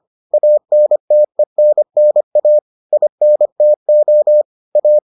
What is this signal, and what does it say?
Morse code, heard as a keyed tone, about 25 wpm, 600 Hz. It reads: ANTENNA INTO A